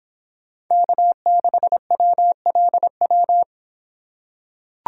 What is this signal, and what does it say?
Morse code, heard as a keyed tone, about 25 wpm, 700 Hz. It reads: K6WLW